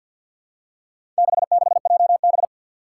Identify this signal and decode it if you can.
Morse code, heard as a keyed tone, about 48 wpm, 700 Hz. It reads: 66JB
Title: Four Challenging Letter-Number Combinations 50wpm